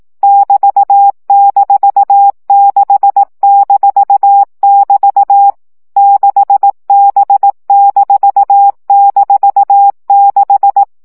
Morse code, heard as a keyed tone, about 18 wpm, 800 Hz. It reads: <BT>-6-<BT> 6B--6
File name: CW gyakorlas b 6 egyenlo minusz 15 baud